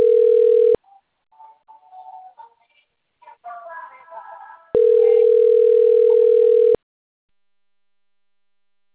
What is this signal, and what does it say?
Ringback tone (North America), 440 + 480 Hz.
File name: rg-1600-unknown- - - .53050